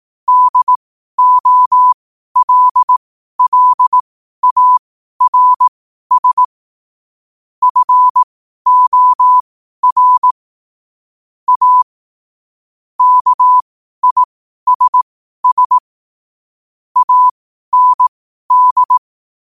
Morse code, heard as a keyed tone, about 18 wpm, spaced as for 12 wpm, 1000 Hz. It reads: DOLLARS FOR A KISS AND
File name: qotdfiles / 12wpm